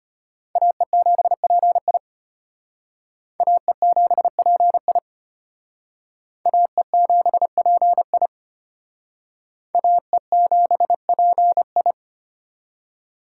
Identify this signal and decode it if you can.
Morse code, heard as a keyed tone, about 32 wpm, 700 Hz. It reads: AE7PI AE7PI AE7PI AE7PI